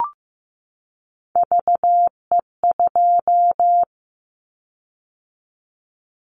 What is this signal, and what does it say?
Morse code, heard as a keyed tone, about 15 wpm, 700 Hz. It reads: VE2
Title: Callsign Prefixes for Top 25 DXCC - 15wpm